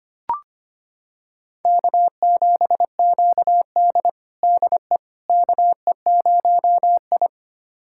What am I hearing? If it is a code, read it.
Morse code, heard as a keyed tone, about 25 wpm, 700 Hz. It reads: K7QD DE KE0I
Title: USA Callsign DE Callsign - 4 Characters - 25wpm